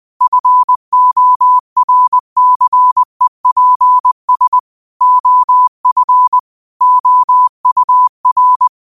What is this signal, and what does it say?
Morse code, heard as a keyed tone, about 20 wpm, 1000 Hz. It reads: FORCEPS OF OUR
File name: qotdfiles / 20wpm